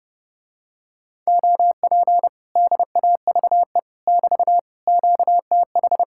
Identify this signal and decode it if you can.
Morse code, heard as a keyed tone, about 30 wpm, 700 Hz. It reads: OP DAVE <BT> QTH